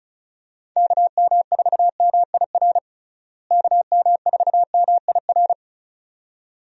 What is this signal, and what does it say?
Morse code, heard as a keyed tone, about 36 wpm, 700 Hz. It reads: KM4MIR KM4MIR
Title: USA Call Signs - 6 Characters - Repeated 2x - 35wpm